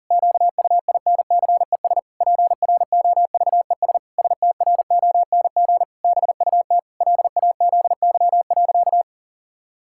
Morse code, heard as a keyed tone, about 40 wpm, 700 Hz. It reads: QUINCES PROVES STRONG BUT LAZY.
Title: Pangram Challenge v2 - 40wpm